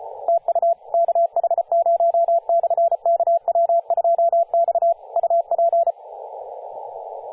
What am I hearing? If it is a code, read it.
Morse code, heard as a keyed tone, about 33 wpm, 680 Hz, with radio noise. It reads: TU KH0/KW2X UP
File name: posts / 2016-sep-22-0537z-kh0-kw2x-18-077mhz-cw